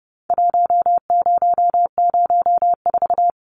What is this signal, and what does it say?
Morse code, heard as a keyed tone, about 30 wpm, 700 Hz. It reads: 1004
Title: Numbers - 4-Digit 30wpm